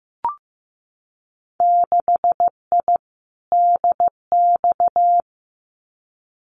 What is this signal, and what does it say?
Morse code, heard as a keyed tone, about 15 wpm, 700 Hz. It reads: 6I DX
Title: Winter Field Day Exchanges Encore 15wpm